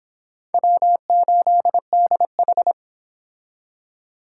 Morse code, heard as a keyed tone, about 25 wpm, 700 Hz. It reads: W8DH